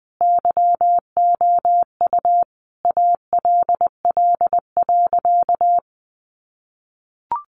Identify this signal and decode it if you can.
Morse code, heard as a keyed tone, about 20 wpm, 700 Hz. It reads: YOU ALL.
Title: Sentences from Top 1100 Words 20wpm